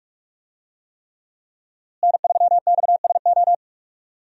Morse code, heard as a keyed tone, about 45 wpm, 700 Hz. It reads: N3XSO
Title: USA Call Signs - 5 Characters - Repeated 3x - 45wpm